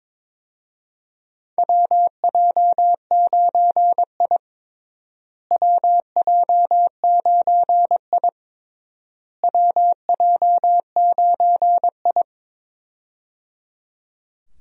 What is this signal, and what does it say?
Morse code, heard as a keyed tone, about 22 wpm, 700 Hz. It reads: WJ9I WJ9I WJ9I
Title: USA Call Signs - 4 Characters - Repeated 3x - 22wpm